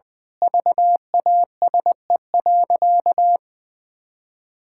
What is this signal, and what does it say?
Morse code, heard as a keyed tone, about 20 wpm, 700 Hz. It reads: VASE.
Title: Pangram Challenge v2 - 20wpm